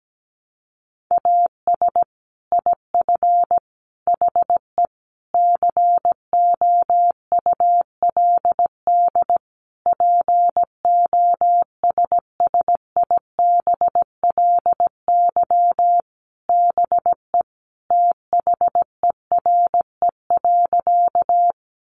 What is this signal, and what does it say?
Morse code, heard as a keyed tone, about 17 wpm, 700 Hz. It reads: AS IF HE COULD POSSIBLY BE THERE.
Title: Sentences from Top 1300 Words 17wpm